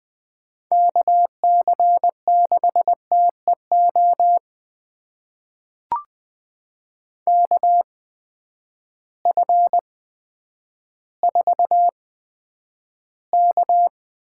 Morse code, heard as a keyed tone, about 20 wpm, spaced as for 10 wpm, 700 Hz. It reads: KC6TEO KF4K
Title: USA Call Signs - 6 Characters - 8x Character Spacing - 20wpm